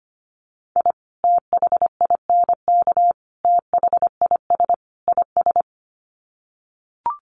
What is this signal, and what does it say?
Morse code, heard as a keyed tone, about 25 wpm, 700 Hz. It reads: I THINK THIS IS
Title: Sets of 4 Words 25wpm